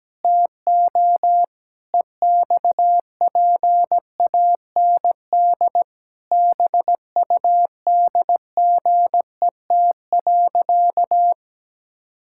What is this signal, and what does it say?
Morse code, heard as a keyed tone, about 17 wpm, 700 Hz. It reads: TO EXPAND BUDGET.
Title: Pangram Challenge v2 - 17wpm